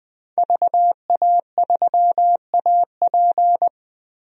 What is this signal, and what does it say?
Morse code, heard as a keyed tone, about 20 wpm, 700 Hz. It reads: VA3AP
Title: DXCC-100 Call Signs - 5 Characters - 20wpm